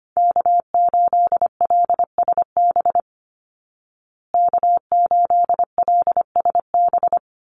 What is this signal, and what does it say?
Morse code, heard as a keyed tone, about 25 wpm, 700 Hz. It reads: K8LSB K8LSB
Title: Call Signs Repeated 3x - 25wpm